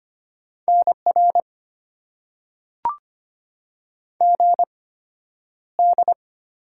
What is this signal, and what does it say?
Morse code, heard as a keyed tone, about 25 wpm, spaced as for 11 wpm, 700 Hz. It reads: NR GD
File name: Top 90 QSO Elements - 8x Character Spacing - 25wpm